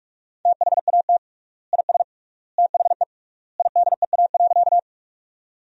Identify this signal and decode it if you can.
Morse code, heard as a keyed tone, about 46 wpm, 700 Hz. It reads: THAT IS THE IDEA.
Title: 3x Word-Spacing - Sentences from Top 300 Words - 45wpm